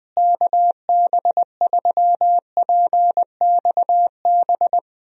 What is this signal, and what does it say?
Morse code, heard as a keyed tone, about 20 wpm, 700 Hz. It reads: KB3PXB
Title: USA Call Signs - 6 Characters - Repeated 2x - 20wpm